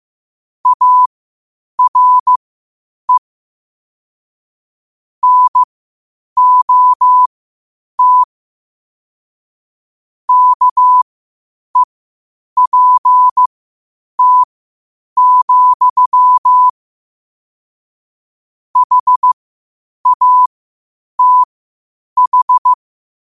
Morse code, heard as a keyed tone, about 15 wpm, spaced as for 8 wpm, 1000 Hz. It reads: ARE NOT KEPT, HATH
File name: qotdfiles / 8wpm